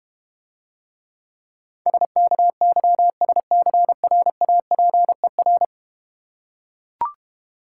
Morse code, heard as a keyed tone, about 31 wpm, 700 Hz. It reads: SKYSCRAPER